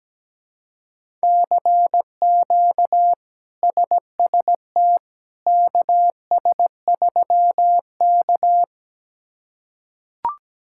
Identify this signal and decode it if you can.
Morse code, heard as a keyed tone, about 17 wpm, 700 Hz. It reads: CQ SST KS3K
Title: K1USN SST QSO Practice - 17wpm